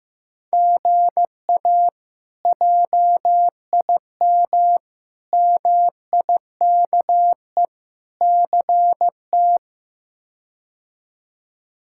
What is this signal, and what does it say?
Morse code, heard as a keyed tone, about 15 wpm, 700 Hz. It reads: GA JIM MIKE CT
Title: K1USN SST QSO Practice - 15wpm